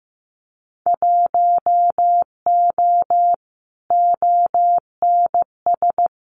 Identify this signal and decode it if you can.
Morse code, heard as a keyed tone, about 15 wpm, 700 Hz. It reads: 1O ONS
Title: Winter Field Day Exchanges Encore 15wpm